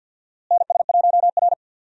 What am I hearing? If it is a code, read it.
Morse code, heard as a keyed tone, about 52 wpm, 700 Hz. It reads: NS1R
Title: USA Call Signs - 4 Characters - Repeated 3x - 50wpm